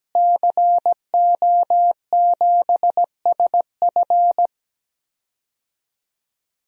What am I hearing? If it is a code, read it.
Morse code, heard as a keyed tone, about 17 wpm, 700 Hz. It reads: CO7SF
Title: DXCC-100 Call Signs - 5 Characters - Repeated 2x - 17wpm